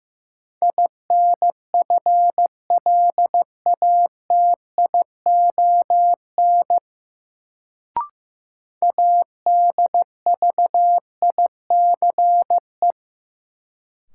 Morse code, heard as a keyed tone, about 15 wpm, 700 Hz. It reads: INFLATION ADVICE
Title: Sentences from Top 1100 Words - New Word Drill - 15wpm